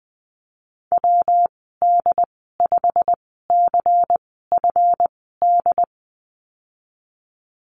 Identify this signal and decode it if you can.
Morse code, heard as a keyed tone, about 20 wpm, spaced as for 15 wpm, 700 Hz. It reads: WD5CFD
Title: 173 Lesson 28g - Number 9 - Call Signs 20wpm